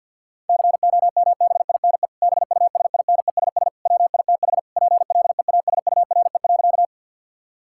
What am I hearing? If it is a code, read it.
Morse code, heard as a keyed tone, about 50 wpm, 700 Hz. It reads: COMBINE BUSINESS WITH PLEASURE.
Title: Sentences from Top 1600 Words 50wpm